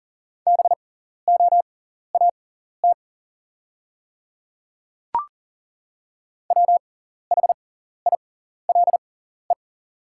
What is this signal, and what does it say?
Morse code, heard as a keyed tone, about 40 wpm, spaced as for 14 wpm, 690 Hz. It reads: BOAT WHILE
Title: 6x Character Spacing - Top 500 Words - 40wpm